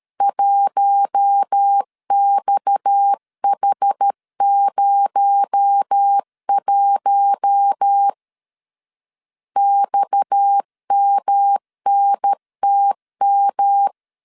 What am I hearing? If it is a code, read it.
Morse code, heard as a keyed tone, about 13 wpm, 800 Hz. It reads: 1XH01 XMNTM
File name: CW / DET / 1 6 1500C S A 12wpm 3440